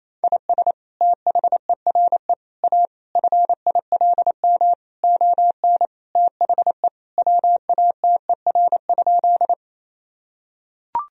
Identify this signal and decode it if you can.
Morse code, heard as a keyed tone, about 28 wpm, 700 Hz. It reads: IS THERE A FILM ON THE WATER?
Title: Sentences from Top 1000 Words - Review Part 3 - 28wpm